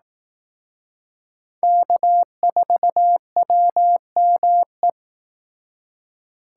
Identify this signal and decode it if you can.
Morse code, heard as a keyed tone, about 18 wpm, 700 Hz. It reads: K4WME
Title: Speed Racing - USA Call Signs 15wpm